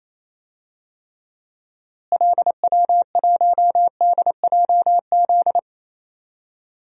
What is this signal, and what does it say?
Morse code, heard as a keyed tone, about 28 wpm, 700 Hz. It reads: LW1DJZ
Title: DXCC-100 Call Signs - 6 Characters - Repeated 2x - 28wpm